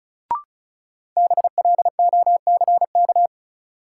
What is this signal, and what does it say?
Morse code, heard as a keyed tone, about 36 wpm, 700 Hz. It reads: BLOCK
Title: Sentences from Top 1600 Words - New Word Drill Alt - 35wpm